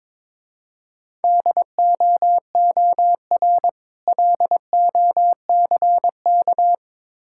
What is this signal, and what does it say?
Morse code, heard as a keyed tone, about 24 wpm, 700 Hz. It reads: DOOR LOCK
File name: Easy Words - Speed-Racing with Hints - 22wpm